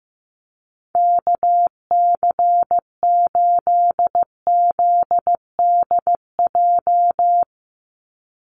Morse code, heard as a keyed tone, about 15 wpm, 700 Hz. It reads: KC8ZDJ
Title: USA Call Signs 15wpm